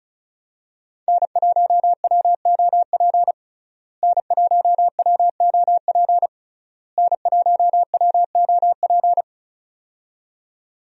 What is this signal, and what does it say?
Morse code, heard as a keyed tone, about 36 wpm, 700 Hz. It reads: N1WOP N1WOP N1WOP
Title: USA Call Signs - 5 Characters - Repeated 3x - 35wpm